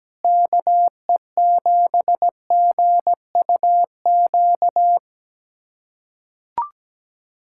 Morse code, heard as a keyed tone, about 17 wpm, 700 Hz. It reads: KE7GUQ
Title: USA Callsign DE Callsign - 6 Characters - 17wpm